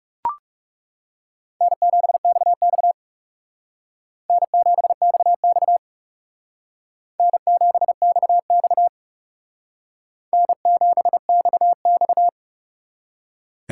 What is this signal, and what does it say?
Morse code, heard as a keyed tone, about 37 wpm, 700 Hz. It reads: N7XX N7XX N7XX N7XX
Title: Speed Racing - USA 1x2 Call Signs 30wpm